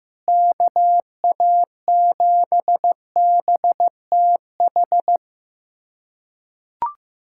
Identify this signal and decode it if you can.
Morse code, heard as a keyed tone, about 15 wpm, 700 Hz. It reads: KA7BTH
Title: USA Callsign DE Callsign 15wpm